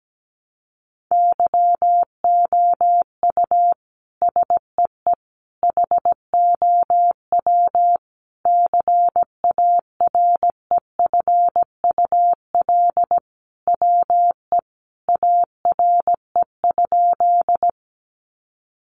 Morse code, heard as a keyed tone, about 17 wpm, 700 Hz. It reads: YOU SEE HOW CAREFUL WE ARE?
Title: A Sound Of Thunder 17wpm